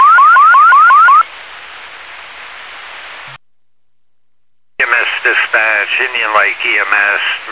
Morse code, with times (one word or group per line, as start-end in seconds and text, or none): none